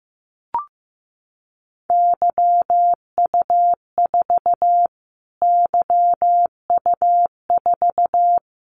Morse code, heard as a keyed tone, about 15 wpm, 700 Hz. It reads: YU4 YU4
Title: Callsign Prefixes for Top 25 DXCC - Repeated 3x - 15wpm